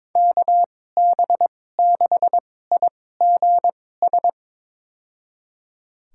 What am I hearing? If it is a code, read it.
Morse code, heard as a keyed tone, about 21 wpm, spaced as for 16 wpm, 700 Hz. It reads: KB6IGS
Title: USA Call Signs - 2x Character Spacing - 22wpm